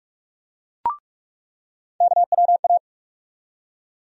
Morse code, heard as a keyed tone, about 42 wpm, 700 Hz. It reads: KWA
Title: USA Call Sign Postfixes - 3 Letters 45wpm